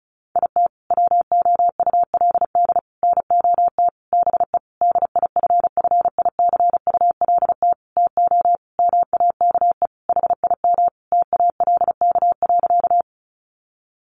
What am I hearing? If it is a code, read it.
Morse code, heard as a keyed tone, about 34 wpm, 700 Hz. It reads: IT WOULD NOT BE DIFFICULT TO MAKE HIM TALK.
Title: Sentences from Top 600 Words 35wpm